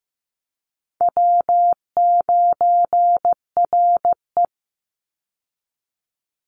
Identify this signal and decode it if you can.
Morse code, heard as a keyed tone, about 15 wpm, 700 Hz. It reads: W9RE
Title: Call Signs Repeated 2x - 15wpm